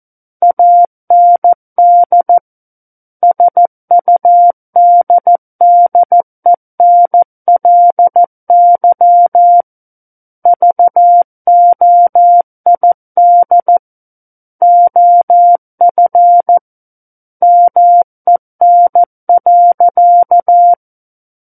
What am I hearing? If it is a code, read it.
Morse code, heard as a keyed tone, about 14 wpm, 700 Hz. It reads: AND SUDDENLY VOID OF MEN.
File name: Book One / 05-Chapter 5 - 14 WPM